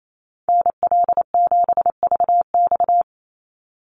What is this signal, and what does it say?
Morse code, heard as a keyed tone, about 28 wpm, 700 Hz. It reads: NL7VX
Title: Call Signs - 28wpm